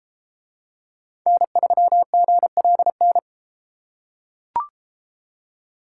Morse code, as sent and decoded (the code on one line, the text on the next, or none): -. ...-- --. .-.. -.
N3GLN